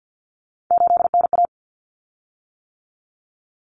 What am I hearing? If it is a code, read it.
Morse code, heard as a keyed tone, about 50 wpm, 700 Hz. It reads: 8NA
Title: Common Cut Numbers - 3-Digit 50wpm